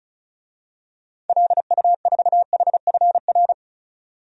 Morse code, heard as a keyed tone, about 34 wpm, 700 Hz. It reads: LU4HFR
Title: DXCC-100 Call Signs - 6 Characters - 35wpm